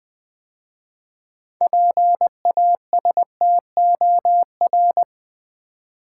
Morse code, heard as a keyed tone, about 21 wpm, 700 Hz. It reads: PASTOR
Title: 6 Letter Words - First 3 Letters Given - 20wpm